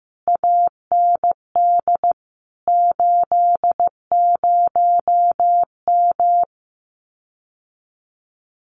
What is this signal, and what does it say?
Morse code, heard as a keyed tone, about 15 wpm, 700 Hz. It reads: AND 80M
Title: Common QSO Phrases Encore 15wpm